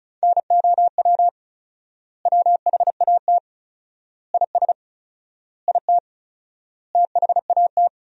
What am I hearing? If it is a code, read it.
Morse code, heard as a keyed tone, about 36 wpm, 700 Hz. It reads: NOW WHAT IS IT THAT
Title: 4x Word-Spacing - Sentences from Top 200 Words - 35wpm